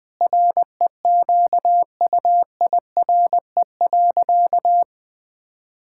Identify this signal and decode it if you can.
Morse code, heard as a keyed tone, about 20 wpm, 700 Hz. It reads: REQUIRE.
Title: Sentences from Top 1000 Words - Review Part 6 - 20wpm